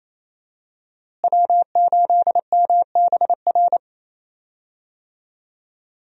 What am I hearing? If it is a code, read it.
Morse code, heard as a keyed tone, about 27 wpm, 700 Hz. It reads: W8MBR
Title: USA Call Signs - 5 Characters - Repeated 2x - 28wpm